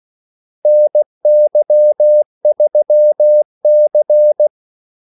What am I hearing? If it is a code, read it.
Morse code, heard as a keyed tone, about 16 wpm, 600 Hz. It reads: NY3C
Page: Kaldesignaler 16 wpm